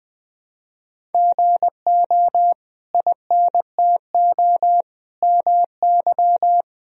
Morse code, heard as a keyed tone, about 21 wpm, 700 Hz. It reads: GO INTO MY
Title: Sentences from Top 1000 Words - Review Part 1 - 20wpm